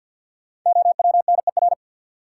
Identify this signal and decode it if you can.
Morse code, heard as a keyed tone, about 52 wpm, 700 Hz. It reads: OWNER
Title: Sentences from Top 1600 Words - New Word Drill - 50wpm